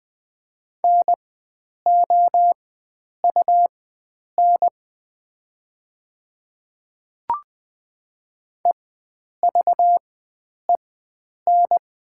Morse code, heard as a keyed tone, about 20 wpm, spaced as for 9 wpm, 690 Hz. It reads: NOUN EVEN
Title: 4x Character Spacing - Top 500 Words - 20wpm